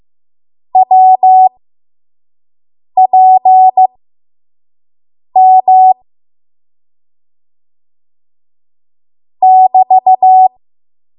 Morse code, heard as a keyed tone, about 15 wpm, spaced as for 5 wpm, 750 Hz. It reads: WPM <BT>
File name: Morse / 05WPM